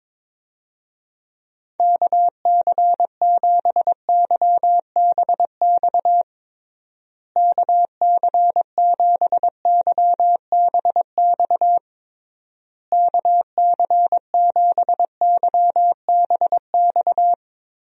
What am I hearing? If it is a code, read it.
Morse code, heard as a keyed tone, about 22 wpm, 700 Hz. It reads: KC7YBX KC7YBX KC7YBX
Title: USA Call Signs - 6 Characters - Repeated 3x - 22wpm